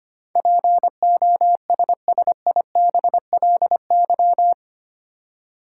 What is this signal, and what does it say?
Morse code, heard as a keyed tone, about 25 wpm, 700 Hz. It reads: POSSIBLY
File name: Eight Letter Words - 25wpm